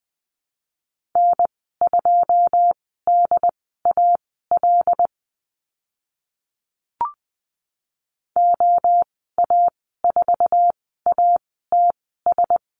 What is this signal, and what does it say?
Morse code, heard as a keyed tone, about 20 wpm, spaced as for 14 wpm, 700 Hz. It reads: N2DAL OA4ATS
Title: 082 Lesson 14g - Number 2 - Call Signs 20wpm